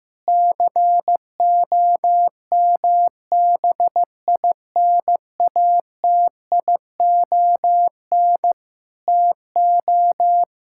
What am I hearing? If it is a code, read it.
Morse code, heard as a keyed tone, about 15 wpm, 700 Hz. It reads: COMBINATION TO
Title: Sentences from Top 1000 Words - Review Part 6 - 15wpm